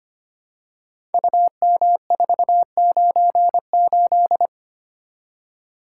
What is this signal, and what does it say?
Morse code, heard as a keyed tone, about 25 wpm, 700 Hz. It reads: UM498
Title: Code Groups - 5 Letter:Numbers - 25wpm